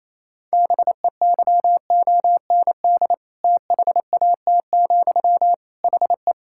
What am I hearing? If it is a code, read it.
Morse code, heard as a keyed tone, about 28 wpm, 700 Hz. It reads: BEYOND THAT, HE
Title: Sentences from Top 1000 Words - Review Part 6 - 28wpm